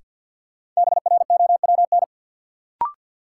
Morse code, heard as a keyed tone, about 50 wpm, 700 Hz. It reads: BROWN
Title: Sentences from Top 1200 Words - New Word Drill - 50wpm